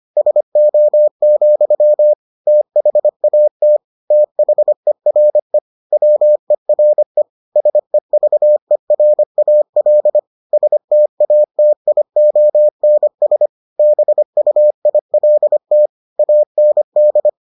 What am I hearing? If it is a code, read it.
Morse code, heard as a keyed tone, about 25 wpm, 600 Hz. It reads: SO, THAT THERE WERE SEVERAL STATIONS BUILT AND